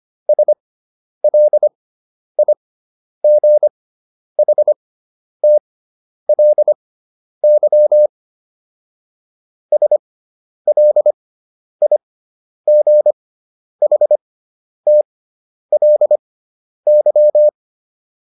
Morse code, heard as a keyed tone, about 25 wpm, spaced as for 10 wpm, 600 Hz. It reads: SLIGHTLY SLIGHTLY